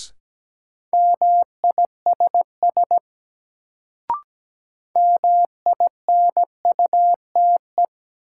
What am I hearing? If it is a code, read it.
Morse code, heard as a keyed tone, about 17 wpm, 700 Hz. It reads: MISS MINUTE
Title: Sentences from Top 500 Words - New Word Drill - 17wpm